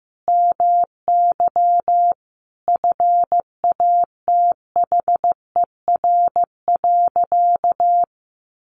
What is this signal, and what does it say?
Morse code, heard as a keyed tone, about 15 wpm, 700 Hz. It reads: MY FATHER.
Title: Sentences from Top 1100 Words 15wpm